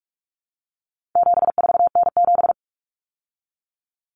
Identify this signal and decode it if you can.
Morse code, heard as a keyed tone, about 44 wpm, 700 Hz. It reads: 74N7
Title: Common Cut Numbers - 4-Digit 45wpm